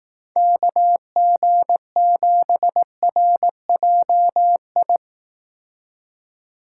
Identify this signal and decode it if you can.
Morse code, heard as a keyed tone, about 18 wpm, 700 Hz. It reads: KG7RJI
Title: Speed Racing - USA 2x3 Call Signs 15wpm